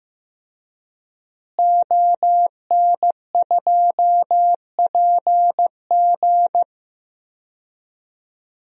Morse code, heard as a keyed tone, about 15 wpm, 700 Hz. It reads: ON2PG
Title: DXCC-100 Call Signs - 5 Characters - Repeated 3x - 15wpm